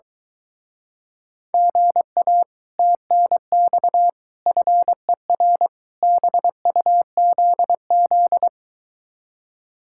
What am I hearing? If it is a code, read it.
Morse code, heard as a keyed tone, about 23 wpm, 700 Hz. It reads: GA TNX FER BUZZ